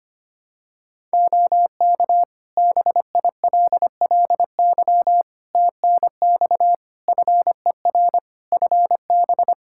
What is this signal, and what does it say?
Morse code, heard as a keyed tone, about 25 wpm, 700 Hz. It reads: OK BILLY TNX FER FB
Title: Practice QSO - Exchange 4 - Ending - 25wpm